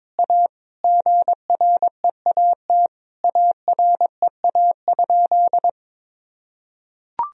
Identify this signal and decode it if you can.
Morse code, heard as a keyed tone, about 22 wpm, 700 Hz. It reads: A GREAT AREA?
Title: Sentences from Top 1000 Words - Review Part 2 - 22wpm